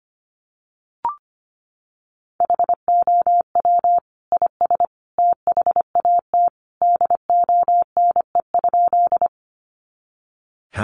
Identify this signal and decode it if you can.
Morse code, heard as a keyed tone, about 25 wpm, 700 Hz. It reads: HOW IS THAT DONE?
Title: Sentences from Top 500 Words 25wpm